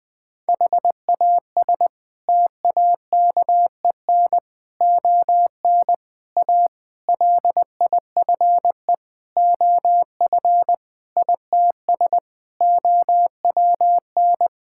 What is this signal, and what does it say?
Morse code, heard as a keyed tone, about 19 wpm, 700 Hz. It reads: HAS TAKEN ON A LIFE OF ITS OWN
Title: Sets of 8 Words - 20wpm